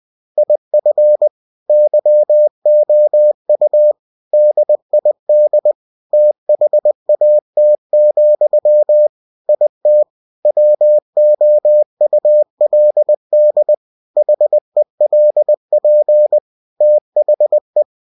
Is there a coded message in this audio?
Yes: Morse code from 0.4 to 17.8 s, heard as a keyed tone, about 20 wpm, 600 Hz. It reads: IF YOU DID THAT, IT WOULD HELP THE